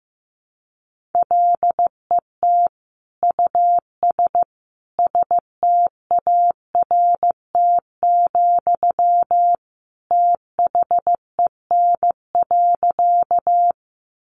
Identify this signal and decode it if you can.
Morse code, heard as a keyed tone, about 15 wpm, 700 Hz. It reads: LET US START, THEN.
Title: Sentences from Top 300 Words 15wpm